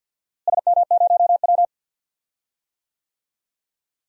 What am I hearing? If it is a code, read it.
Morse code, heard as a keyed tone, about 51 wpm, 700 Hz. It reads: SM0W
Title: DXCC-100 Call Signs - 4 Characters - Repeated 3x - 50wpm